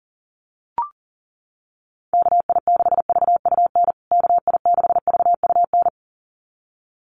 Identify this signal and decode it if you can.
Morse code, heard as a keyed tone, about 37 wpm, 700 Hz. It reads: KI6VUN KI6VUN
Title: USA Call Signs Repeated 2x 40wpm